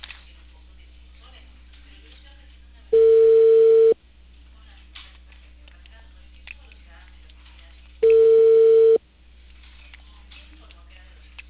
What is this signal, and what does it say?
Ringback tone (China, 450 Hz), with speech.